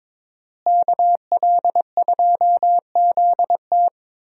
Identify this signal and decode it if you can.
Morse code, heard as a keyed tone, about 22 wpm, 700 Hz. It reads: KL2ZT
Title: USA Call Signs - 5 Characters - Repeated 2x - 22wpm